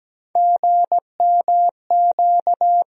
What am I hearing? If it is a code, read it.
Morse code, heard as a keyed tone, about 17 wpm, 700 Hz. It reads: GMQ
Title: USA Call Sign Postfixes - 3 Letters 17wpm